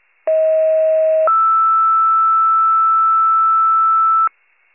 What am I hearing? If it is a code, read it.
Two-tone page, heard as a one-second tone, then a three-second tone, about 640 then 1320 Hz. It is heard from 0.3 to 4.3 s.